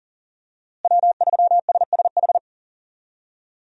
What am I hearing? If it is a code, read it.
Morse code, heard as a keyed tone, about 37 wpm, 700 Hz. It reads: W3SSH
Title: USA Call Signs - 5 Characters - Repeated 2x - 40wpm